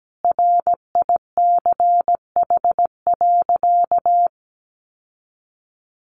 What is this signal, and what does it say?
Morse code, heard as a keyed tone, about 17 wpm, 700 Hz. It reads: RICH.
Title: Sentences from Top 1100 Words 17wpm